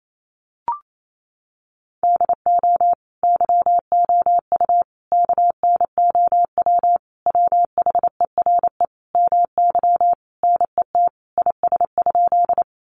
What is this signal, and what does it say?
Morse code, heard as a keyed tone, about 28 wpm, 700 Hz. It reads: DO YOU KNOW WHERE MY NET IS?
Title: Sentences from Top 1400 Words 28wpm